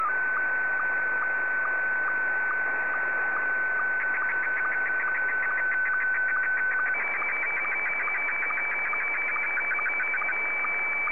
SSTV (slow-scan television), Scottie 1.
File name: sstv / S1-VK2UBQ